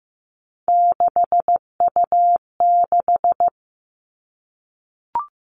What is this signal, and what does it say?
Morse code, heard as a keyed tone, about 15 wpm, 700 Hz. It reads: 6U6
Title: All Cut Numbers - 3-Digit 15wpm